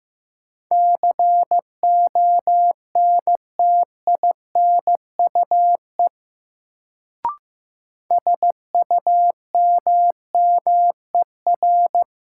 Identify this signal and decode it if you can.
Morse code, heard as a keyed tone, about 15 wpm, 700 Hz. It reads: CONTINUE SUMMER
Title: Sentences from Top 1000 Words - Review Part 6 - New Word Drill - 15wpm